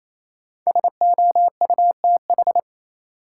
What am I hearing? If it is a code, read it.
Morse code, heard as a keyed tone, about 28 wpm, 700 Hz. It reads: SOUTH
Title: Sentences from Top 400 Words - New Word Drill - 28wpm